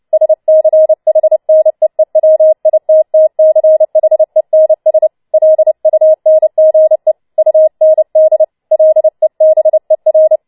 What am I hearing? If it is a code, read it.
Morse code, heard as a keyed tone, about 29 wpm, 630 Hz. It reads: SCHNEEWITTCHENS LUNGE UND LEBER
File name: Schneewittchen Teil3 30wpm